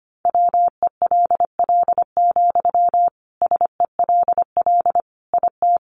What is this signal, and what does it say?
Morse code, heard as a keyed tone, about 25 wpm, 700 Hz. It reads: WELL, SELL IT
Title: Sentences from Top 1100 Words 25wpm